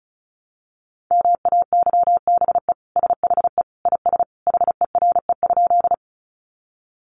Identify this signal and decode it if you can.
Morse code, heard as a keyed tone, about 34 wpm, 700 Hz. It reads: MAYBE SHE IS HERE?
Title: Sentences from Top 1000 Words 35wpm